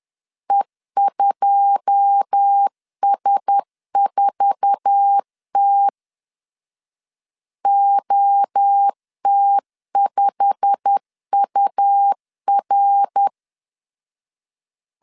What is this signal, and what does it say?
Morse code, heard as a keyed tone, about 10 wpm, 800 Hz. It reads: E2S4T OT5UR
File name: LEZIONI / 9 2 10wpm 2736